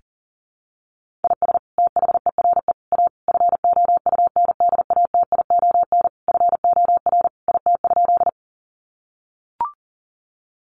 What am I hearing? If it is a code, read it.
Morse code, heard as a keyed tone, about 40 wpm, 700 Hz. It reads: IS THERE A FOUNDATION FOR IT?
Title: Sentences from Top 1200 Words 40wpm